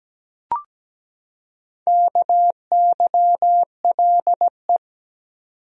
Morse code, heard as a keyed tone, about 17 wpm, 700 Hz. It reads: KYLE
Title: USA Common Names - Max 4 Letters - 17wpm